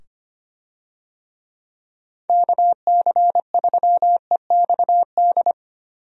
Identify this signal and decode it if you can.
Morse code, heard as a keyed tone, about 25 wpm, 700 Hz. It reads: KC3EXD
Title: USA Call Signs - 6 Characters - 25wpm